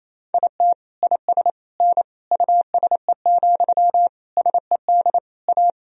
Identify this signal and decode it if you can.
Morse code, heard as a keyed tone, about 28 wpm, 700 Hz. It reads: IT IS N USE, SED A
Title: Sentences from Top 500 Words and Abbreviations 28wpm